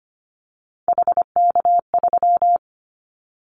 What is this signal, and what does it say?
Morse code, heard as a keyed tone, about 25 wpm, 700 Hz. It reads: HK3
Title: Callsign Prefixes for Top 100 DXCC - 25wpm